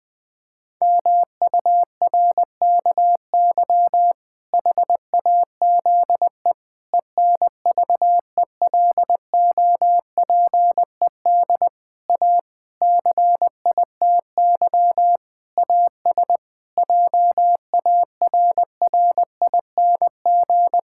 Morse code, heard as a keyed tone, about 21 wpm, 700 Hz. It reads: MURKY HAZE ENVELOPED A CITY AS JARRING
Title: Pangram Challenge v2 - 20wpm